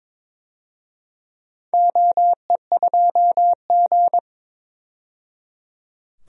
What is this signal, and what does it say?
Morse code, heard as a keyed tone, about 21 wpm, 700 Hz. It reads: OE2G
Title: DXCC-100 Call Signs - 4 Characters - 22wpm